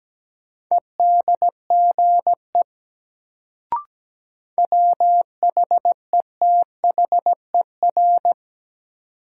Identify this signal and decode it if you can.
Morse code, heard as a keyed tone, about 17 wpm, 700 Hz. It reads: EDGE WHETHER
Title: Sets of 3 Words - New Word Drill - 17wpm